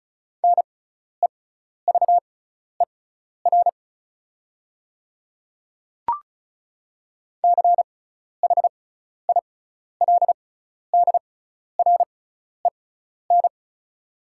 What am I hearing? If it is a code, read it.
Morse code, heard as a keyed tone, about 36 wpm, spaced as for 12 wpm, 690 Hz. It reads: NEVER CHILDREN
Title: 6x Character Spacing - Top 500 Words - 35wpm